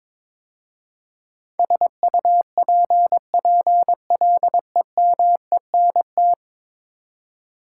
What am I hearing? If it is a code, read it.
Morse code, heard as a keyed tone, about 22 wpm, 700 Hz. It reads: SUPPLEMENT
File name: practice / 22wpm / 10 Letter Words - First 3 Letters Given - 22wpm